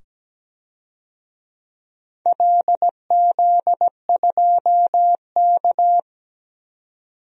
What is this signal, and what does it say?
Morse code, heard as a keyed tone, about 17 wpm, 700 Hz. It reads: LZ2K
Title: DXCC-100 Call Signs - 4 Characters - 17wpm